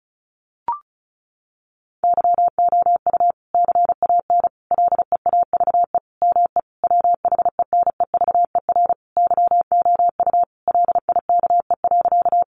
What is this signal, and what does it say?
Morse code, heard as a keyed tone, about 34 wpm, 700 Hz. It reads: YOU CAN LEAVE ME WHENEVER YOU LIKE.
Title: Sentences from Top 1200 Words 35wpm